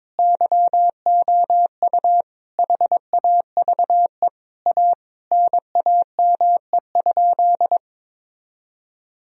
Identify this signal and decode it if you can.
Morse code, heard as a keyed tone, about 23 wpm, 700 Hz. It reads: YOU HAVE A NAME?
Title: Sentences from Top 1000 Words - Review Part 3 - 22wpm